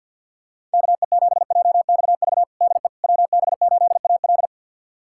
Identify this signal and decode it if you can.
Morse code, heard as a keyed tone, about 46 wpm, 700 Hz. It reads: KE7JXV DE WB8AL
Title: USA Callsign DE Callsign 50wpm